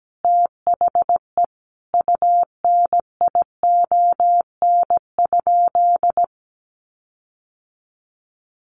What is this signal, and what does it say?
Morse code, heard as a keyed tone, about 17 wpm, 700 Hz. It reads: THE UNION?
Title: Sentences from Top 1400 Words 17wpm